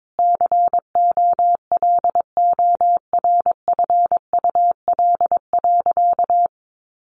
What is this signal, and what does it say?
Morse code, heard as a keyed tone, about 22 wpm, 700 Hz. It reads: COLORFUL.
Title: Easy Phrases 22wpm